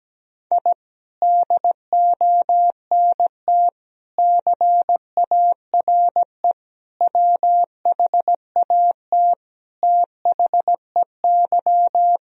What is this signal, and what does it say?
Morse code, heard as a keyed tone, about 17 wpm, 700 Hz. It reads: I DONT CARE WHAT THEY
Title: Sentences from Top 1000 Words - Review Part 1 - 17wpm